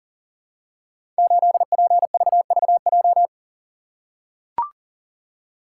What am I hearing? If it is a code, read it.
Morse code, heard as a keyed tone, about 42 wpm, 700 Hz. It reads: 8PVVJ
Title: Code Groups - 5 Letter:Numbers - 40wpm